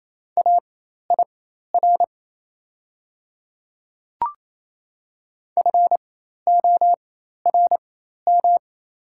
Morse code, heard as a keyed tone, about 28 wpm, spaced as for 13 wpm, 700 Hz. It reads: AIR FORM
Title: 4x Character Spacing - Top 500 Words - 28wpm